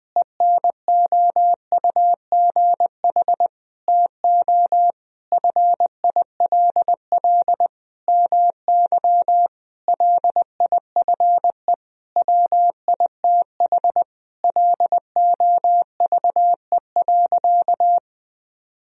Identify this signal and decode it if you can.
Morse code, heard as a keyed tone, about 20 wpm, 700 Hz. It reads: ENOUGH TO FILL MY LIFE WITH LOVE.
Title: Sentences from Top 1000 Words - Review Part 3 - 20wpm